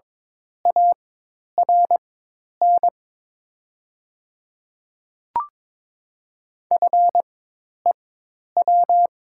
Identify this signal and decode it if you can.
Morse code, heard as a keyed tone, about 22 wpm, spaced as for 10 wpm, 700 Hz. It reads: ARN FEW
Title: 4x Character Spacing - Top 300 Words - 22wpm